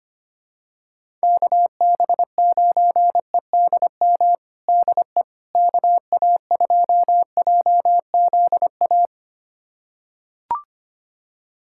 Morse code, heard as a keyed tone, about 25 wpm, 700 Hz. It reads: KB9EDM DE KA2JZA
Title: USA Callsign DE Callsign 25wpm